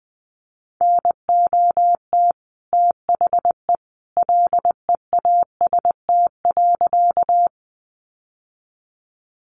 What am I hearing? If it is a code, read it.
Morse code, heard as a keyed tone, about 19 wpm, 700 Hz. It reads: NOT THE LEAST.
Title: Sentences from Top 700 Words 20wpm